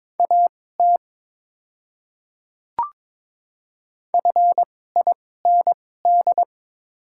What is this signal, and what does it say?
Morse code, heard as a keyed tone, about 22 wpm, spaced as for 16 wpm, 700 Hz. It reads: AT FIND
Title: 2x Character Spacing - Top 100 Words - 22wpm